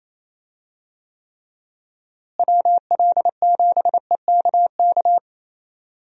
Morse code, heard as a keyed tone, about 28 wpm, 700 Hz. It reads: WL7EKK
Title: USA Call Signs - 6 Characters - Repeated 3x - 28wpm